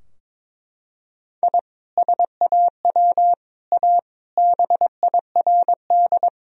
Morse code, heard as a keyed tone, about 22 wpm, 700 Hz. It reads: I SAW A BIRD
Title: Sentences from Top 1000 Words - Review Part 2 - 22wpm